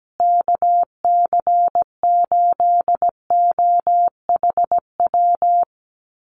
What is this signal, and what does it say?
Morse code, heard as a keyed tone, about 17 wpm, 700 Hz. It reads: KC8OHW
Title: USA Call Signs Repeated 3x 17wpm